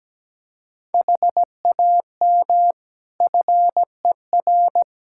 Morse code, heard as a keyed tone, about 17 wpm, 700 Hz. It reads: HAM FER
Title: K1USN SST QSO Practice 17wpm